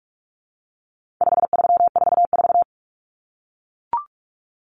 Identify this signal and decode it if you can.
Morse code, heard as a keyed tone, about 44 wpm, 700 Hz. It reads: 5344
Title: Common Cut Numbers - 4-Digit 45wpm